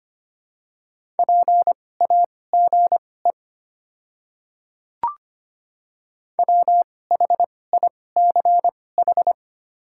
Morse code, heard as a keyed tone, about 25 wpm, spaced as for 18 wpm, 700 Hz. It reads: PAGE WHICH
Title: 2x Character Spacing - Top 300 Words - 25wpm